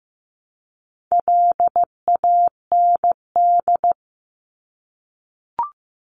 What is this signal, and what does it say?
Morse code, heard as a keyed tone, about 15 wpm, 700 Hz. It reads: LAND
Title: Top 500 Words 15wpm